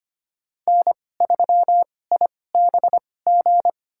Morse code, heard as a keyed tone, about 25 wpm, spaced as for 18 wpm, 700 Hz. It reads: N3IBG
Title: USA Call Signs - 2x Character Spacing - 25wpm